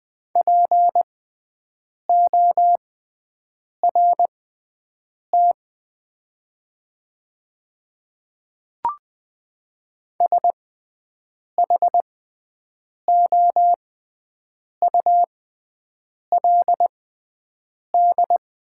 Morse code, heard as a keyed tone, about 20 wpm, spaced as for 7 wpm, 700 Hz. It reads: PORT SHOULD
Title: 6x Character Spacing - Top 500 Words - 20wpm